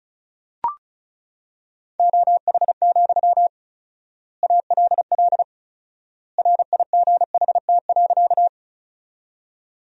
Morse code, heard as a keyed tone, about 34 wpm, 700 Hz. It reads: OH, ALL RIGHT.
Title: 4x Word-Spacing - Sentences from Top 500 Words - 35wpm